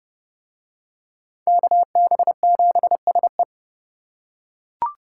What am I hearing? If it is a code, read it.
Morse code, heard as a keyed tone, about 30 wpm, 700 Hz. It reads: KB7SE